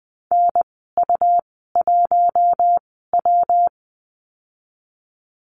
Morse code, heard as a keyed tone, about 20 wpm, spaced as for 15 wpm, 700 Hz. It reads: NU1W
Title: 128 Lesson 21g - Number 6 - Call Signs 20wpm